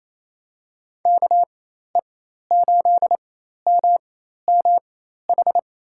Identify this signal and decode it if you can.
Morse code, heard as a keyed tone, about 28 wpm, spaced as for 13 wpm, 700 Hz. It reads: KE8MMH
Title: USA Call Signs - 6 Characters - 4x Character Spacing - 28wpm